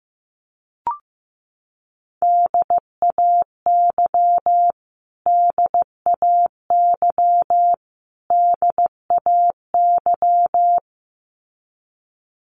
Morse code, heard as a keyed tone, about 15 wpm, 700 Hz. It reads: DAY DAY DAY
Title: Top 300 Words Repeated 15wpm